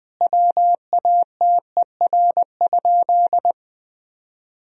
Morse code, heard as a keyed tone, about 21 wpm, 700 Hz. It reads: WATER?
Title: Sentences from Top 1000 Words - Review Part 3 - 20wpm